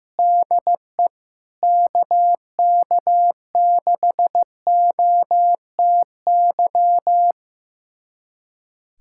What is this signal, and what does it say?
Morse code, heard as a keyed tone, about 15 wpm, 700 Hz. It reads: DE KK6OTY
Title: USA Callsign DE Callsign - 6 Characters - 15wpm